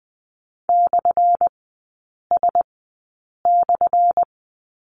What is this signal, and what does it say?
Morse code, heard as a keyed tone, about 20 wpm, 700 Hz. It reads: / S /
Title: 178 Lesson 29e - Slash - Three Characters 20wpm